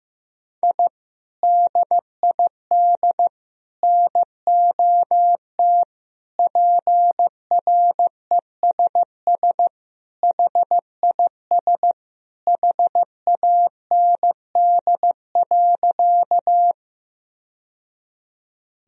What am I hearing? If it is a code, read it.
Morse code, heard as a keyed tone, about 15 wpm, 700 Hz. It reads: I DID NOT PRESS HIS HAND.
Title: Sentences from Top 1000 Words - Review Part 1 - 15wpm